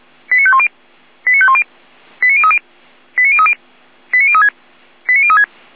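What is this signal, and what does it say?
Five-tone selective call (ZVEI), heard as quick runs of five tones, tones of about 70 ms. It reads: 89510 89510 89030 89030 89036 89036
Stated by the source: ZVEI five-tone call received by SDR.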